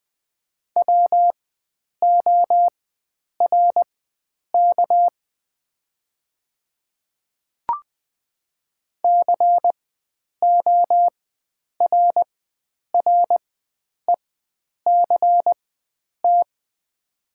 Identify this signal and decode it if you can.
Morse code, heard as a keyed tone, about 20 wpm, spaced as for 9 wpm, 700 Hz. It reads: WORK CORRECT
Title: 4x Character Spacing - Top 500 Words - 20wpm